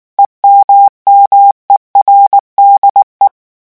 Morse code, heard as a keyed tone, about 19 wpm, 800 Hz. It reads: EMMERDE
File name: morse mp3 / revision9 citations humoristiques vitesse 18 mots minute